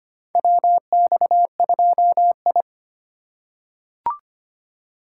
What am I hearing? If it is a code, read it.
Morse code, heard as a keyed tone, about 25 wpm, 700 Hz. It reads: WX2I
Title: USA Call Signs - 3x Character Spacing - 25wpm